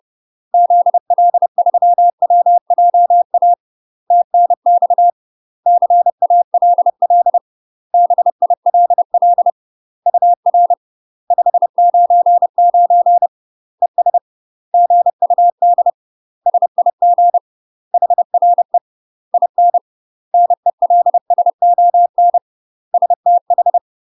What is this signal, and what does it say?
Morse code, heard as a keyed tone, about 30 wpm, 700 Hz. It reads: ZL3WJA TNX CALL BILL UR 599 ES GUD SIG HRE IN NELSON STH
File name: UK MASTER QSOs TRAINING / 30WPM / UK MASTER QSOs 0018